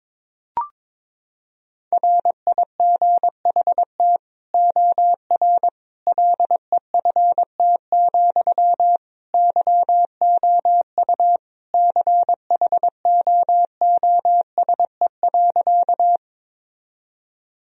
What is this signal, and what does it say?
Morse code, heard as a keyed tone, about 22 wpm, 700 Hz. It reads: RIGHT OR LEFT, YOU CHOOSE.
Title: Sentences from Top 1000 Words - Review Part 4 - 22wpm